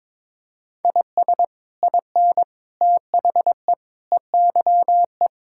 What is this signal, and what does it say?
Morse code, heard as a keyed tone, about 22 wpm, 700 Hz. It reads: IS IN THE EYE
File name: practice / 22wpm / Sets of 7 Words - 22wpm